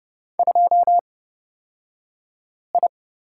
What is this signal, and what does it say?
Morse code, heard as a keyed tone, about 30 wpm, spaced as for 5 wpm, 690 Hz. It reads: 2I
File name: Lesson 36d - BT - Two Characters 30wpm